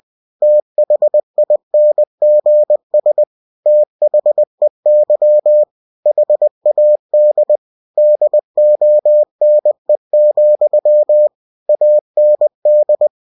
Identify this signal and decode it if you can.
Morse code, heard as a keyed tone, about 20 wpm, 600 Hz. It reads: THINGS THEY HAD DONE, AND
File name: cw-books / winnie-the-pooh / winnie-20wpm-page-0101